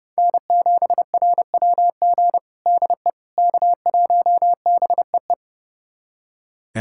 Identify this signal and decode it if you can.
Morse code, heard as a keyed tone, about 30 wpm, 700 Hz. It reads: N7RWG DE K1BEE
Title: USA Callsign DE Callsign - 5 Characters - 30wpm